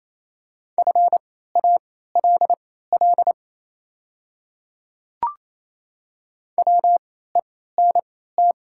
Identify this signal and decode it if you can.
Morse code, heard as a keyed tone, about 28 wpm, spaced as for 16 wpm, 700 Hz. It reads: FALL WENT
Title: 3x Character Spacing - Top 500 Words - 28wpm